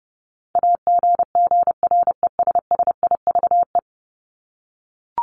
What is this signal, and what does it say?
Morse code, heard as a keyed tone, about 29 wpm, 700 Hz. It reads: AGGRESSIVE
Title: Top 1501-1750 Words 30wpm